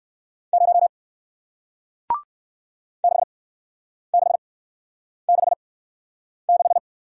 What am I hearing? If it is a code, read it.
Morse code, heard as a keyed tone, about 58 wpm, spaced as for 9 wpm, 700 Hz. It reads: 0 6666